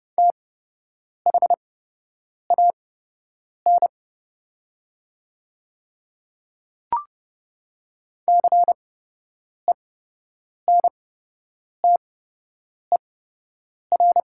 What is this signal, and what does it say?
Morse code, heard as a keyed tone, about 30 wpm, spaced as for 8 wpm, 700 Hz. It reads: THAN CENTER
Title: 8x Character Spacing - Top 500 Words - 30wpm